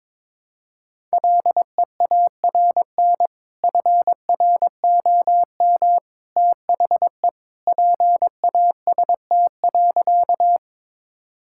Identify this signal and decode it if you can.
Morse code, heard as a keyed tone, about 21 wpm, 700 Hz. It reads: LEARN FROM THE PAST.
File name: practice / 22wpm / Sentences with 2-Word Phrases - 22wpm